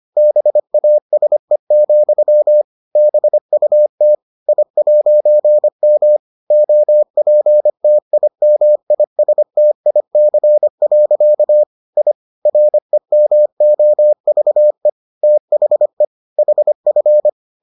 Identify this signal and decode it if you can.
Morse code, heard as a keyed tone, about 25 wpm, 600 Hz. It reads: BASE, BUT I'M OPTIMISTIC. I REMOVE THE HF